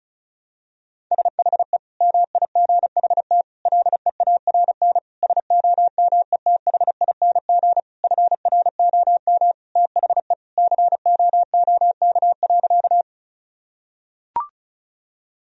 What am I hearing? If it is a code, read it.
Morse code, heard as a keyed tone, about 34 wpm, 700 Hz. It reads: SHE MIGHT LEARN SOMETHING FROM THE COOK.
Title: Sentences from Top 1000 Words - Review Part 4 - 35wpm